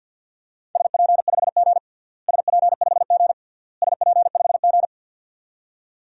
Morse code, heard as a keyed tone, about 52 wpm, 700 Hz. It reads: SP5G SP5G SP5G
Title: DXCC-100 Call Signs - 4 Characters - Repeated 3x - 50wpm